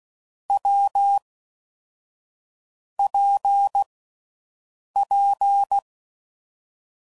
Morse code, heard as a keyed tone, about 16 wpm, 800 Hz. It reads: W P P
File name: Combinations / Difficult Combos / WPJ WPJ1